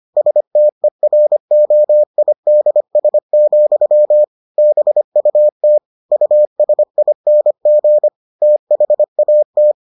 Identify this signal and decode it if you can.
Morse code, heard as a keyed tone, about 25 wpm, 600 Hz. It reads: STEROIDS, BUT USING THAT